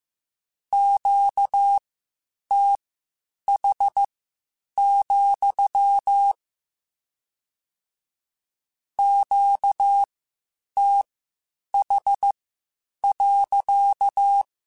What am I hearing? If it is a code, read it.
Morse code, heard as a keyed tone, about 15 wpm, spaced as for 8 wpm, 800 Hz. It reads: QTH, QTH.
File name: Q Calls / Q Calls-mp3s / Q Calls sppc @13